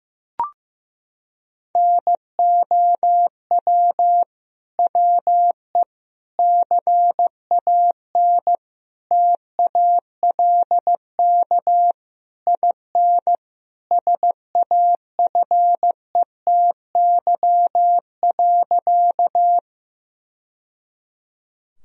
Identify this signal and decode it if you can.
Morse code, heard as a keyed tone, about 16 wpm, 700 Hz. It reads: NOW WE CAN TALK IN SAFETY.
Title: Sentences from Top 1000 Words - Review Part 4 - 15wpm